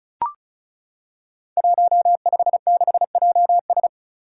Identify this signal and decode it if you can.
Morse code, heard as a keyed tone, about 36 wpm, 700 Hz. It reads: 156JS
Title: Five Challenging Letter-Number Combinations 35wpm